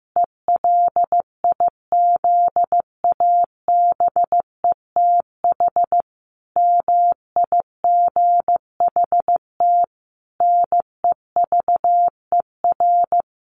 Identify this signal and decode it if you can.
Morse code, heard as a keyed tone, about 15 wpm, 700 Hz. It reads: ELIZABETH MIGHT NEVER
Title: A Sound Of Thunder 15wpm